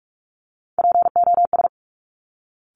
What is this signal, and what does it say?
Morse code, heard as a keyed tone, about 46 wpm, 700 Hz. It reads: POS
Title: Top Three Letter Combinations in Words 45wpm